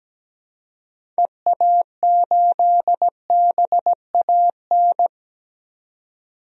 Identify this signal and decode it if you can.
Morse code, heard as a keyed tone, about 17 wpm, 700 Hz. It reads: EA8BAN
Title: DXCC-100 Call Signs - 6 Characters - 17wpm